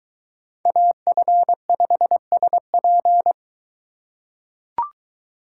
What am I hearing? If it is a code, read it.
Morse code, heard as a keyed tone, about 23 wpm, 700 Hz. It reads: AF5SP